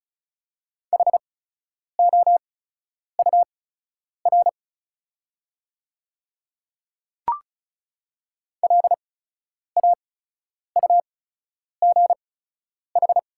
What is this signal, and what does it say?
Morse code, heard as a keyed tone, about 34 wpm, spaced as for 10 wpm, 690 Hz. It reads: HOUR LAUGH
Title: 8x Character Spacing - Top 500 Words - 35wpm